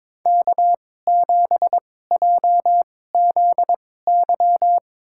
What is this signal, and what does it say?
Morse code, heard as a keyed tone, about 22 wpm, spaced as for 16 wpm, 700 Hz. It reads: K7JZY
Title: USA Call Signs - 2x Character Spacing - 22wpm